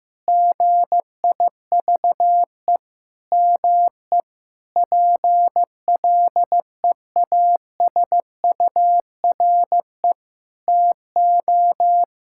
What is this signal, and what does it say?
Morse code, heard as a keyed tone, about 15 wpm, 700 Hz. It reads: GIVE ME PLEASURE TO
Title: Sentences from Top 1600 Words 15wpm